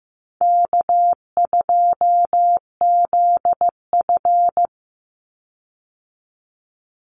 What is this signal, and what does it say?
Morse code, heard as a keyed tone, about 15 wpm, 700 Hz. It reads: K2ZF
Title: Easy Call Signs Repeated 3x - 15wpm